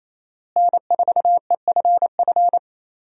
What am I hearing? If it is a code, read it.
Morse code, heard as a keyed tone, about 28 wpm, 700 Hz. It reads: N4EFF
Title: USA Call Signs - 5 Characters - Repeated 3x - 28wpm